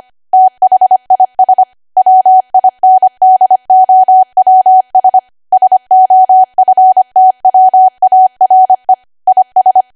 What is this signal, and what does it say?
Morse code, heard as a keyed tone, about 25 wpm, 750 Hz. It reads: THIS WINDOWS SOFTWARE IS
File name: Archive / 25 WPM / 25WPM